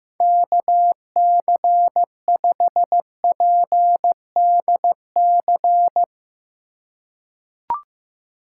Morse code, heard as a keyed tone, about 15 wpm, 700 Hz. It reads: KC5PDC
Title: USA Call Signs - 6 Characters - 2x Character Spacing - 15wpm